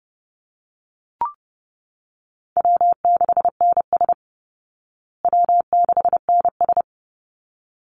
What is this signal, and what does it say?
Morse code, heard as a keyed tone, about 30 wpm, 700 Hz. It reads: W6NS W6NS
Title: Easy Call Signs Repeated 2x - 30wpm